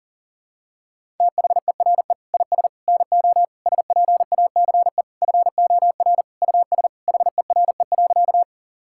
Morse code, heard as a keyed tone, about 37 wpm, 700 Hz. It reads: THERE IS NO SPACE FOR US HERE.
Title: Sentences from Top 1000 Words - Review Part 5 - 40wpm